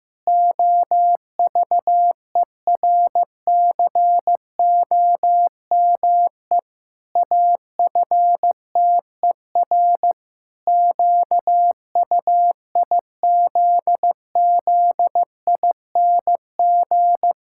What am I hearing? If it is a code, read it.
Morse code, heard as a keyed tone, about 15 wpm, 700 Hz. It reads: OVERCOME AFTER QUIZZING
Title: Pangram Challenge v2 - 15wpm